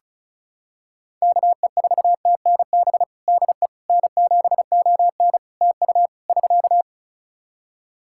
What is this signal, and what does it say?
Morse code, heard as a keyed tone, about 36 wpm, 700 Hz. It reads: KE4TNB DE N7ON TU <SK>
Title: Practice QSO - Exchange 4 - Ending - 35wpm